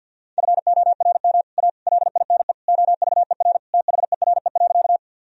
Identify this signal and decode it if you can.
Morse code, heard as a keyed tone, about 47 wpm, 700 Hz. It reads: FORM A LINE OVER THERE.
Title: Sentences from Top 1000 Words - Review Part 1 - 50wpm